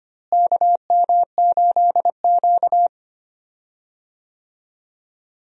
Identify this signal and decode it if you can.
Morse code, heard as a keyed tone, about 25 wpm, 700 Hz. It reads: KM8Q
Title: USA Call Signs - 4 Characters - Repeated 2x - 25wpm